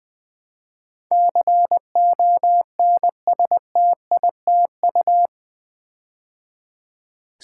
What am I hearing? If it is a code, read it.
Morse code, heard as a keyed tone, about 20 wpm, 700 Hz. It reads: CONSTITU
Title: Build A Word - 4 Letters At A Time - 10-Letter Words - 20wpm